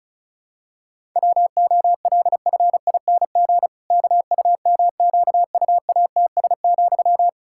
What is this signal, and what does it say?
Morse code, heard as a keyed tone, about 36 wpm, 700 Hz. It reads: WOLFING KUMQUATS,
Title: Pangram Challenge v2 - 35wpm